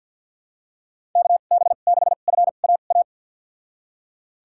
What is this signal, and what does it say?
Morse code, heard as a keyed tone, about 52 wpm, spaced as for 37 wpm, 700 Hz. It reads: KB6FAA
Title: USA Call Signs - 2x Character Spacing - 50wpm